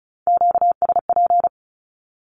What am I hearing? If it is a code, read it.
Morse code, heard as a keyed tone, about 36 wpm, 700 Hz. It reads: QSP
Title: Q-Codes with Spoken Meaning - 35wpm